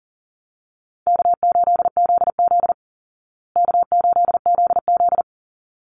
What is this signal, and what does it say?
Morse code, heard as a keyed tone, about 40 wpm, 700 Hz. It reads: K8ZZ K8ZZ
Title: Easy Call Signs Repeated 2x - 40wpm